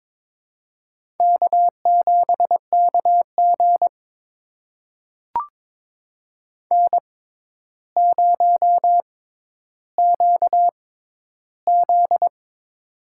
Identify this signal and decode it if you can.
Morse code, heard as a keyed tone, about 22 wpm, spaced as for 11 wpm, 700 Hz. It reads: K7KG N0QZ
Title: USA Call Signs - 4 Characters - 6x Character Spacing - 22wpm